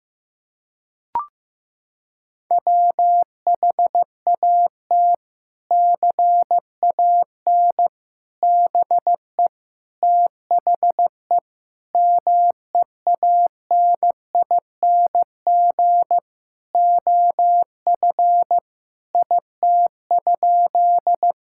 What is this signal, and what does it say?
Morse code, heard as a keyed tone, about 15 wpm, 700 Hz. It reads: WHAT CAN BE THE MEANING OF IT?
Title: Sentences from Top 1000 Words - Review Part 6 - 15wpm